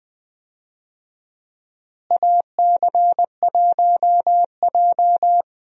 Morse code, heard as a keyed tone, about 20 wpm, 700 Hz. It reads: AC1J
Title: USA Call Signs - 4 Characters - Repeated 3x - 20wpm